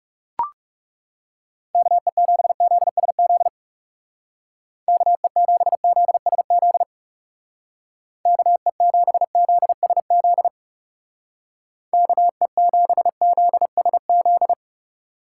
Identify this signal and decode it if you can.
Morse code, heard as a keyed tone, about 38 wpm, 700 Hz. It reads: KE7ZSZ KE7ZSZ KE7ZSZ KE7ZSZ